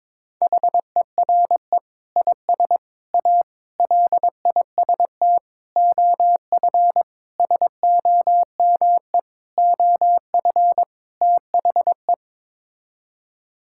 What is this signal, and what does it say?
Morse code, heard as a keyed tone, about 22 wpm, 700 Hz. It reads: HERE IS A LIST OF SOME OF THE
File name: practice / 22wpm / Sets of 8 Words - 22wpm